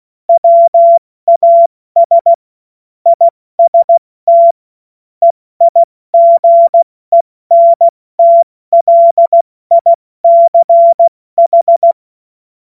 Morse code, heard as a keyed tone, about 16 wpm, 680 Hz. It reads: WAS IST EIGENTLICH
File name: Morsekurs / Witze / Witz 14. Februar 2023 70 F80